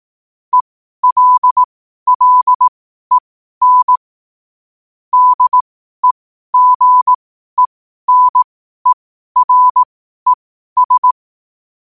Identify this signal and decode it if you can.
Morse code, heard as a keyed tone, about 18 wpm, spaced as for 12 wpm, 1000 Hz. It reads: ELLEN DEGENERES
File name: qotdfiles / 12wpm